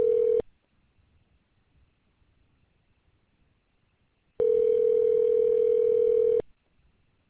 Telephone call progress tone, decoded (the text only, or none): ringback tone